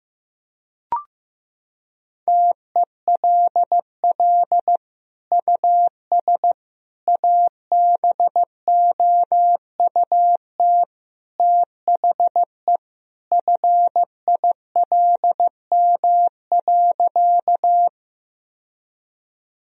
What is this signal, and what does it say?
Morse code, heard as a keyed tone, about 15 wpm, 700 Hz. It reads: TELL US ABOUT THE FILM.
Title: Sentences from Top 1000 Words - Review Part 3 - 15wpm